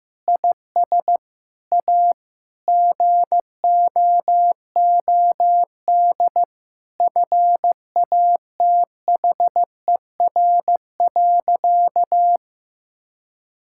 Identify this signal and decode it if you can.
Morse code, heard as a keyed tone, about 15 wpm, 700 Hz. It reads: IS A GOOD FATHER.
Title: Sentences from Top 1000 Words - Review Part 5 - 15wpm